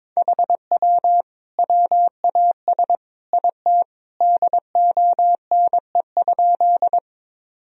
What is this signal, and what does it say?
Morse code, heard as a keyed tone, about 23 wpm, 700 Hz. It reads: HW WAS IT DONE?
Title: Sentences from Top 500 Words and Abbreviations 22wpm